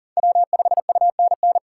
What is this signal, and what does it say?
Morse code, heard as a keyed tone, about 40 wpm, 700 Hz. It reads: W5UNN
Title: USA Call Signs - 5 Characters - 4x Character Spacing - 40wpm